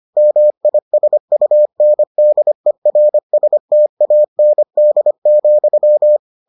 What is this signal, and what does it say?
Morse code, heard as a keyed tone, about 25 wpm, 600 Hz. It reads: MISUNDERSTAND,